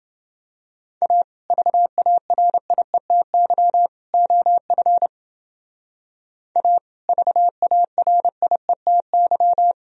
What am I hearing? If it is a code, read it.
Morse code, heard as a keyed tone, about 28 wpm, 700 Hz. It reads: A VARIETY OF A VARIETY